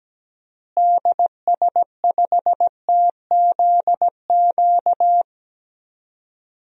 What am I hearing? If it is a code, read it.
Morse code, heard as a keyed tone, about 17 wpm, 700 Hz. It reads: DS5TZQ
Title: DXCC-100 Call Signs - 6 Characters - Repeated 3x - 17wpm